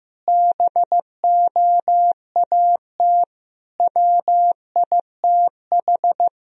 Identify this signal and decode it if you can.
Morse code, heard as a keyed tone, about 15 wpm, 700 Hz. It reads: BOAT WITH
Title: Sentences from Top 1000 Words - Review Part 3 - 15wpm